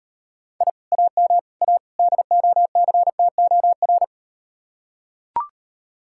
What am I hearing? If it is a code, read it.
Morse code, heard as a keyed tone, about 38 wpm, 700 Hz. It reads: I AM A DOCTOR